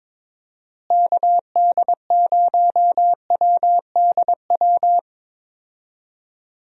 Morse code, heard as a keyed tone, about 22 wpm, 700 Hz. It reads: KD0WDW
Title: USA Call Signs - 6 Characters - Repeated 3x - 22wpm